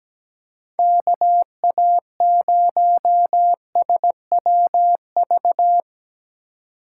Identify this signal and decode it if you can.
Morse code, heard as a keyed tone, about 19 wpm, 700 Hz. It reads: KA0SWV
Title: Speed Racing - USA 2x3 Call Signs 17wpm